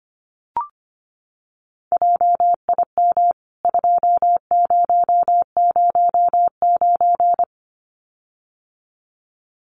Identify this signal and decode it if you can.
Morse code, heard as a keyed tone, about 25 wpm, 700 Hz. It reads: JIM 2009
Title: CWT Exchange 25wpm